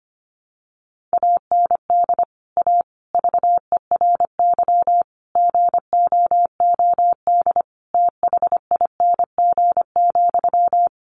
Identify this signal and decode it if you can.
Morse code, heard as a keyed tone, about 25 wpm, 700 Hz. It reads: AND A VERY GOOD THING,
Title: Sentences from Top 200 Words 25wpm